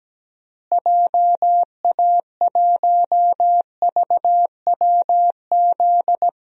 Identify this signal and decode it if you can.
Morse code, heard as a keyed tone, about 17 wpm, 700 Hz. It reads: JA1VWZ
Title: DXCC-100 Call Signs - 6 Characters - Repeated 3x - 17wpm